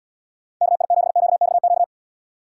Morse code, heard as a keyed tone, about 77 wpm, 700 Hz. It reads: KI7QXQ